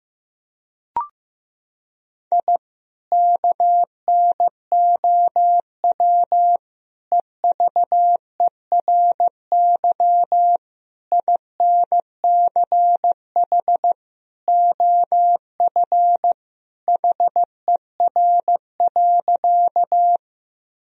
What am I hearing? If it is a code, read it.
Morse code, heard as a keyed tone, about 15 wpm, 700 Hz. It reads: I KNOW EVERY INCH OF HER.
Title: Sentences from Top 1000 Words - Review Part 3 - 15wpm